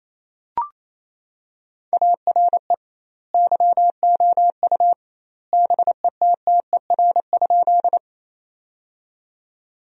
Morse code, heard as a keyed tone, about 28 wpm, 700 Hz. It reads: ARE YOU BETTER?
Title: 2x Word-Spacing - Sentences from Top 400 Words - 28wpm